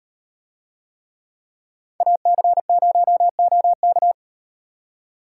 Morse code, heard as a keyed tone, about 38 wpm, 700 Hz. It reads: AC0OK